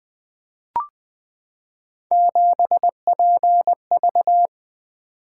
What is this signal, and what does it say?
Morse code, heard as a keyed tone, about 20 wpm, 700 Hz. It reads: 7PV
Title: Three Letter-Number Combinations - 20wpm